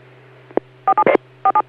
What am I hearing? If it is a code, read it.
DTMF, heard as telephone touch tones, tones of about 70 ms. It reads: 47 47